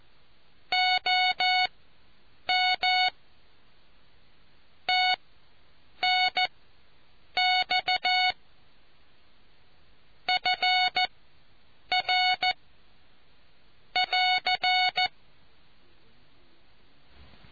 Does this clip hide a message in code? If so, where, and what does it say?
Yes: Morse code from 0.7 to 15.1 s, heard as a keyed tone, about 14 wpm, spaced as for 7 wpm, 740 Hz. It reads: OM TNX FR <AR>